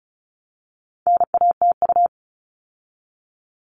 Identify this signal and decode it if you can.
Morse code, heard as a keyed tone, about 34 wpm, 700 Hz. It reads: NATU
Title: All Cut Numbers - 4-Digit 35wpm